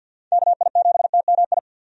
Morse code, heard as a keyed tone, about 51 wpm, 700 Hz. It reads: KI7TMI
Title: USA Call Signs - 6 Characters - 50wpm